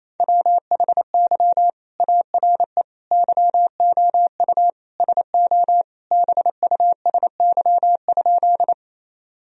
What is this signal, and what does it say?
Morse code, heard as a keyed tone, about 28 wpm, 700 Hz. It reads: WHY ARE YOU SO BUSY?
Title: Sentences from Top 1000 Words - Review Part 3 - 28wpm